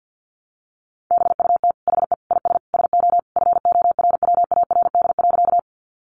Morse code, heard as a keyed tone, about 50 wpm, 700 Hz. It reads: BUT HE IS SO FORWARD.
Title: Sentences from Top 1300 Words 50wpm